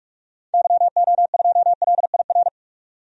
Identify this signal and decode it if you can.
Morse code, heard as a keyed tone, about 46 wpm, 700 Hz. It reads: YO2LIR
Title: DXCC-100 Call Signs - 6 Characters - 45wpm